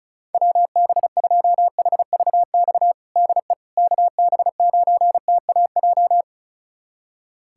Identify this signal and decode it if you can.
Morse code, heard as a keyed tone, about 34 wpm, 700 Hz. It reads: WB2HVX DE KB9TAJ
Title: USA Callsign DE Callsign - 6 Characters - 35wpm